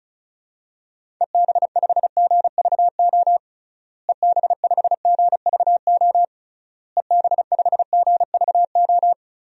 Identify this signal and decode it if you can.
Morse code, heard as a keyed tone, about 36 wpm, 700 Hz. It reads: EB5GVO EB5GVO EB5GVO
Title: DXCC-100 Call Signs - 6 Characters - Repeated 3x - 35wpm